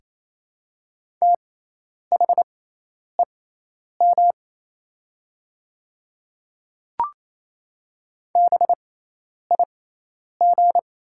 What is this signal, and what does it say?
Morse code, heard as a keyed tone, about 28 wpm, spaced as for 10 wpm, 700 Hz. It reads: THEM BIG
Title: 6x Character Spacing - Top 300 Words - 28wpm